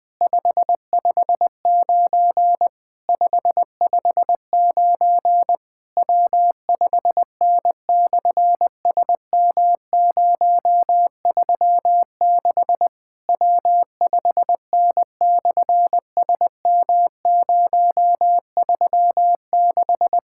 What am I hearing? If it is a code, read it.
Morse code, heard as a keyed tone, about 20 wpm, 700 Hz. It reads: 559 559 W5N/SM036 W5N/SM036
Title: SOTA QSO Practice 20wpm